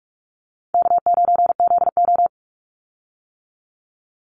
Morse code, heard as a keyed tone, about 44 wpm, 700 Hz. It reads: K9ZO
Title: Call Signs - 45wpm